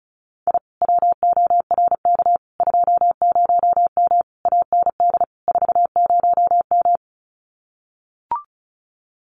Morse code, heard as a keyed tone, about 34 wpm, 700 Hz. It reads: I WORK 20M AND 40M
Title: Common QSO Phrases Encore 35wpm